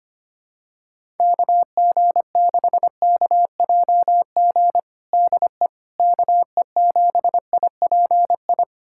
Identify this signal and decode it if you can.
Morse code, heard as a keyed tone, about 25 wpm, 700 Hz. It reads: KG6KJG DE KE7IPI
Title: USA Callsign DE Callsign - 6 Characters - 25wpm